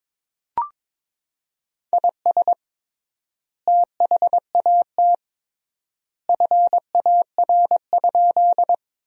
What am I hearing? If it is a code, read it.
Morse code, heard as a keyed tone, about 22 wpm, 700 Hz. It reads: IS THAT FAR?
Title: 3x Word-Spacing - Sentences from Top 300 Words - 22wpm